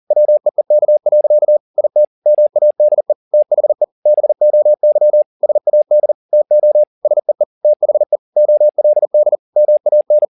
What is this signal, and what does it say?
Morse code, heard as a keyed tone, about 40 wpm, 600 Hz. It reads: WEEK. IT MADE THE BOY SAD TO SEE THE OLD MAN